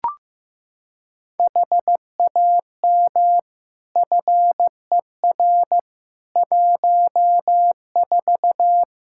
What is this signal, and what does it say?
Morse code, heard as a keyed tone, about 15 wpm, 700 Hz. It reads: HAM FER 14
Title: Practice QSO - Exchange 3 - More Info - 15wpm